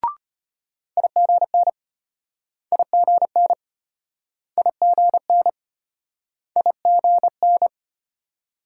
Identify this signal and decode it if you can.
Morse code, heard as a keyed tone, about 34 wpm, 700 Hz. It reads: IGN IGN IGN IGN